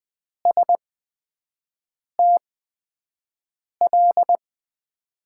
Morse code, heard as a keyed tone, about 21 wpm, spaced as for 5 wpm, 700 Hz. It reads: STL
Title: USA Call Signs - 6 Characters - 8x Character Spacing - 20wpm